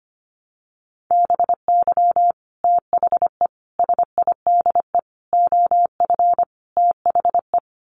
Morse code, heard as a keyed tone, about 25 wpm, 700 Hz. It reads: BY THE SIDE OF THE
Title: Sets of 5 Words Encore2 25wpm